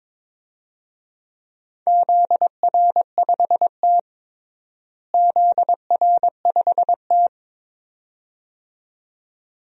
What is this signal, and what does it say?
Morse code, heard as a keyed tone, about 21 wpm, 700 Hz. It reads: ZR5T ZR5T
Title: DXCC-100 Call Signs - 4 Characters - Repeated 2x - 22wpm